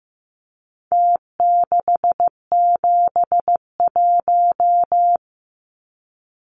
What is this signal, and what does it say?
Morse code, heard as a keyed tone, about 15 wpm, 700 Hz. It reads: T671
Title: Numbers - Contest Style - 4-Digit 15wpm